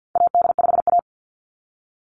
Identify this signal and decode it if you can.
Morse code, heard as a keyed tone, about 50 wpm, 700 Hz. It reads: AD5A
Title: NA SOTA Chaser Callsigns 50wpm